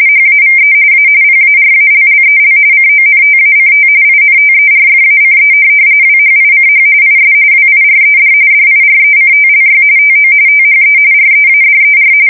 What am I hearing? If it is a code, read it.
RTTY, about 45.45 baud. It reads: HAVE A PARALLEL CONTACT INTERFACE WHICH IS WIRED DIRECTLY TO THE READER
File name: fsk170 / teletype-corp